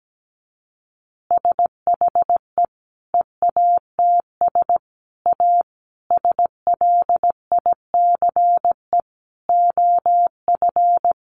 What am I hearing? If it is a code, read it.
Morse code, heard as a keyed tone, about 17 wpm, 700 Hz. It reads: SHE EATS A SLICE OF
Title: Additional Easy Phrases 17wpm